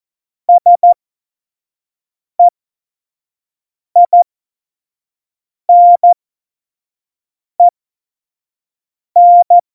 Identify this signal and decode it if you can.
Morse code, heard as a keyed tone, about 14 wpm, spaced as for 5 wpm, 700 Hz. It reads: SEINEN
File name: Morsekurs / Witze / Witz 19. Januar 2021, 25 F70